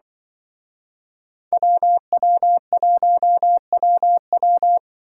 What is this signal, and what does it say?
Morse code, heard as a keyed tone, about 23 wpm, 700 Hz. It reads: WW1WW
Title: Speed Racing - USA 2x2 Call Signs 20wpm